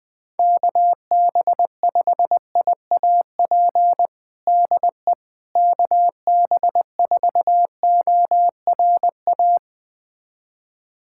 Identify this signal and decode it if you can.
Morse code, heard as a keyed tone, about 21 wpm, 700 Hz. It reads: KB5IAP DE KB4ORA
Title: USA Callsign DE Callsign - 6 Characters - 20wpm